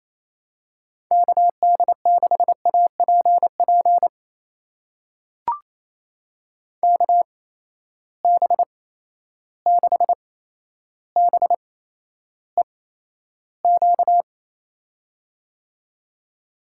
Morse code, heard as a keyed tone, about 28 wpm, spaced as for 12 wpm, 700 Hz. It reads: KD6APP KB6BEQ
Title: USA Call Signs - 6 Characters - 8x Character Spacing - 28wpm